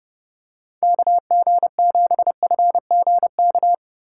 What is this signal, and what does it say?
Morse code, heard as a keyed tone, about 32 wpm, 700 Hz. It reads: KG7FGK